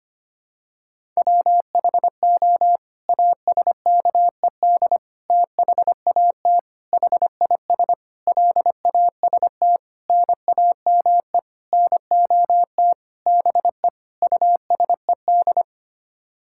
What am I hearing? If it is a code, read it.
Morse code, heard as a keyed tone, about 25 wpm, 700 Hz. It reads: WHO ASKED THAT HIS LAST NAME NOT BE USED
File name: Sets of 9 Words - 25wpm